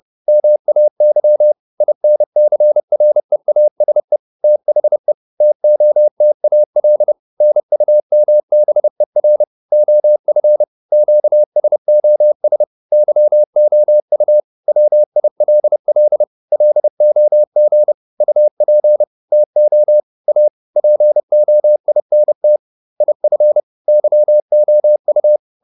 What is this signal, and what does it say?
Morse code, heard as a keyed tone, about 30 wpm, 600 Hz. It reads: MAY INCREASE THE TOTAL NUMBER OF QSOS YOU WILL LOG UP TO A POINT IF YOU